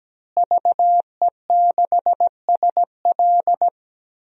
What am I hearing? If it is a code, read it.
Morse code, heard as a keyed tone, about 17 wpm, 700 Hz. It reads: VE6SL
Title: DXCC-100 Call Signs - 5 Characters - Repeated 2x - 17wpm